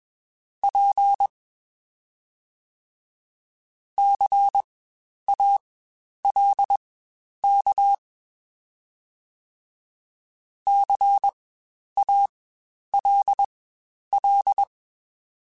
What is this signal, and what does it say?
Morse code, heard as a keyed tone, about 21 wpm, spaced as for 10 wpm, 800 Hz. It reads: P CALK CALL
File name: Spaced Alphas / Spaced Alphas mp3s / Cs sp @20